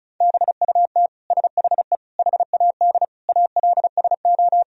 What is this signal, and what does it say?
Morse code, heard as a keyed tone, about 34 wpm, 700 Hz. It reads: BUT SHE HAD ALSO
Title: Sentences from Top 1000 Words - Review Part 1 - 35wpm